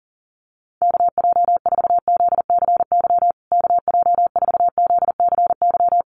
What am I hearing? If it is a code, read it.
Morse code, heard as a keyed tone, about 40 wpm, 700 Hz. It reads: KJ4ZCY KJ4ZCY
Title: USA Call Signs Repeated 2x 40wpm